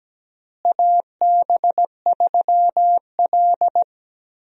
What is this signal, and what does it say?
Morse code, heard as a keyed tone, about 17 wpm, 700 Hz. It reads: AB3L
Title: USA Call Signs - 4 Characters - Repeated 2x - 17wpm